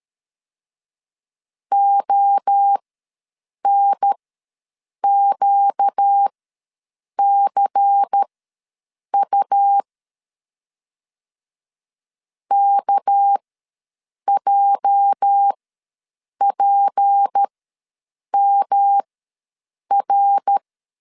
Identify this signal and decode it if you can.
Morse code, heard as a keyed tone, about 13 wpm, spaced as for 7 wpm, 800 Hz. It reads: ONQCU KJPMR
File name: LEZIONI / 8 YJ45 / 8 YJ ws 12wpm 3342